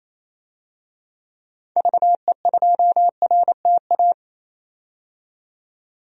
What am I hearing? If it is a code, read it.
Morse code, heard as a keyed tone, about 28 wpm, 700 Hz. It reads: VE2RTA
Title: DXCC-100 Call Signs - 6 Characters - 28wpm